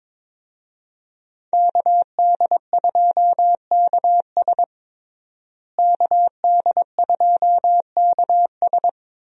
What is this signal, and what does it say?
Morse code, heard as a keyed tone, about 22 wpm, 700 Hz. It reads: KD2KS KD2KS
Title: USA Call Signs - 5 Characters - Repeated 3x - 22wpm